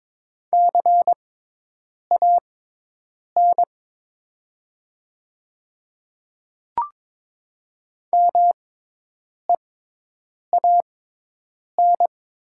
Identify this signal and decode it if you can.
Morse code, heard as a keyed tone, about 22 wpm, spaced as for 8 wpm, 700 Hz. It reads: CAN MEAN
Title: 6x Character Spacing - Top 300 Words - 22wpm